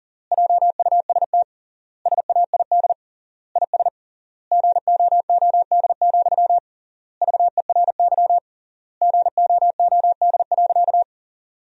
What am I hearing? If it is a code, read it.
Morse code, heard as a keyed tone, about 40 wpm, 700 Hz. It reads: JUST SAID IS GOOD, VERY GOOD.
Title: 3x Word-Spacing - Sentences from Top 200 Words - 40wpm